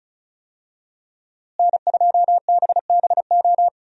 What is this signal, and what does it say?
Morse code, heard as a keyed tone, about 36 wpm, 700 Hz. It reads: N2BBO
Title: USA Call Signs - 5 Characters - Repeated 2x - 35wpm